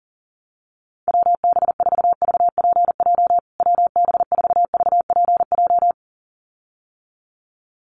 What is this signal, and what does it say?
Morse code, heard as a keyed tone, about 40 wpm, 700 Hz. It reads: WB4VPJ WB4VPJ
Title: USA Call Signs Repeated 2x 40wpm